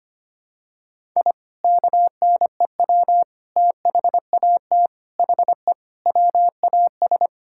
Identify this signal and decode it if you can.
Morse code, heard as a keyed tone, about 25 wpm, 700 Hz. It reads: I KNEW THAT HE WAS
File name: practice / 25wpm / Sentences with 3-Word Phrases - 25wpm